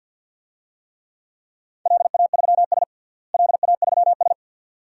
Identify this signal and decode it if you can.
Morse code, heard as a keyed tone, about 52 wpm, 700 Hz. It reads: LA3S LA3S
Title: DXCC-100 Call Signs - 4 Characters - Repeated 2x - 50wpm